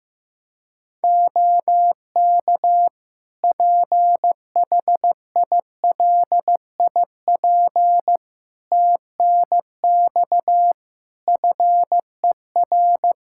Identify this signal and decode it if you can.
Morse code, heard as a keyed tone, about 15 wpm, 700 Hz. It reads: OK PHILIP TNX FER
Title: Practice QSO - Exchange 4 - Ending - 15wpm